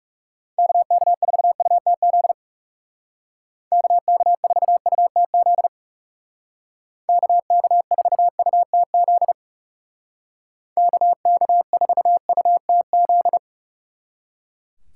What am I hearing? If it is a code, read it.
Morse code, heard as a keyed tone, about 36 wpm, 700 Hz. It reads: KK4UTZ KK4UTZ KK4UTZ KK4UTZ
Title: Speed Racing - USA 2x3 Call Signs 30wpm